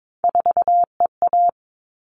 Morse code, heard as a keyed tone, about 22 wpm, 700 Hz. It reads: 4EA
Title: All Cut Numbers - 3-Digit 22wpm